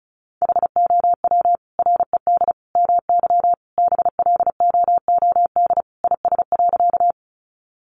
Morse code, heard as a keyed tone, about 34 wpm, 700 Hz. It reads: HOW RED MY BLOOD IS.
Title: Sentences from Top 1100 Words 35wpm